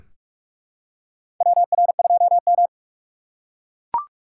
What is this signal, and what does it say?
Morse code, heard as a keyed tone, about 41 wpm, 690 Hz. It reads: WR2M
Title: USA Call Signs - 4x Character Spacing - 45wpm